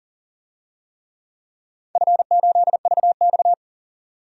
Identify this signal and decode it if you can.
Morse code, heard as a keyed tone, about 38 wpm, 700 Hz. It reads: F8VX
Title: DXCC-100 Call Signs - 4 Characters - 40wpm